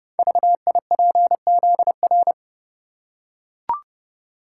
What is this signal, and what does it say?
Morse code, heard as a keyed tone, about 30 wpm, 700 Hz. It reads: VIPZR
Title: Code Groups - 5 Letters - 30wpm